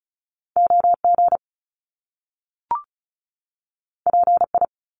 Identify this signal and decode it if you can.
Morse code, heard as a keyed tone, about 36 wpm, 700 Hz. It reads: O G P I
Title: Top Two Letter Combinations in Words 35wpm